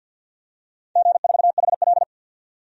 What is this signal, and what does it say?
Morse code, heard as a keyed tone, about 52 wpm, 700 Hz. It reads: G4HL
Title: DXCC-100 Call Signs - 4 Characters - 50wpm